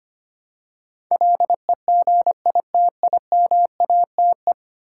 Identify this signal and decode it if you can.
Morse code, heard as a keyed tone, about 25 wpm, 700 Hz. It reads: LEGITIMATE
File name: Ten Letter Words - 25wpm